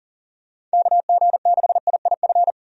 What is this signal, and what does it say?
Morse code, heard as a keyed tone, about 40 wpm, 700 Hz. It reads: KG6IIF
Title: USA Call Signs - 6 Characters - 40wpm